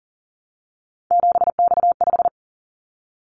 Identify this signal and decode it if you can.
Morse code, heard as a keyed tone, about 42 wpm, 700 Hz. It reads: 7X5
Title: Callsign Prefixes for Top 100 DXCC - 40wpm